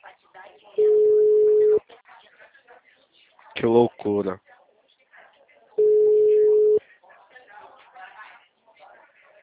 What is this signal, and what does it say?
Ringback tone (Europe, 425 Hz), with speech.
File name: out- -2018- - - .28593